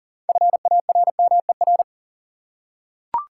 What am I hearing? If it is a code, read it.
Morse code, heard as a keyed tone, about 41 wpm, 700 Hz. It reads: FARMER
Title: Six Letter Words - 40wpm